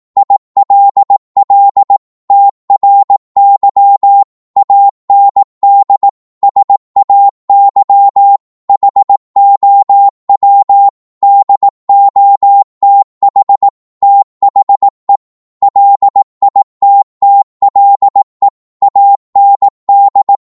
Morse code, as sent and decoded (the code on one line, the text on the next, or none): .. .-.. .-.. / - .-. -.-- / .- -. -.. / ... .- -.-- / .... --- .-- / -.. --- - .... / - .... / .-.. .. - - .-.. / .- -. -..
ILL TRY AND SAY HOW DOTH THE LITTLE AND